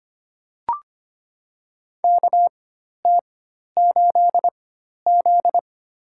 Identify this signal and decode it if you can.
Morse code, heard as a keyed tone, about 25 wpm, spaced as for 12 wpm, 700 Hz. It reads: KT8Z
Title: USA Call Signs - 4 Characters - 4x Character Spacing - 25wpm